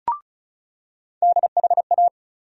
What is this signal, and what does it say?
Morse code, heard as a keyed tone, about 36 wpm, 700 Hz. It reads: DHA
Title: USA Call Sign Postfixes - 3 Letters 35wpm